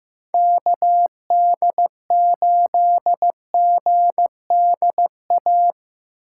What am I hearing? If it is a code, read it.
Morse code, heard as a keyed tone, about 18 wpm, 700 Hz. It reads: KD8GDA
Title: Speed Racing - USA Call Signs 15wpm